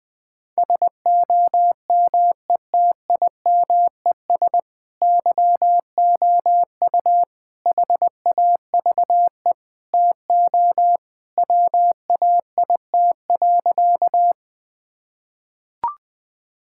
Morse code, heard as a keyed tone, about 21 wpm, 700 Hz. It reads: SOMETIMES YOU HAVE TO WAIT.
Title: Sentences with 3-Word Phrases - Spoken Beginning - 20wpm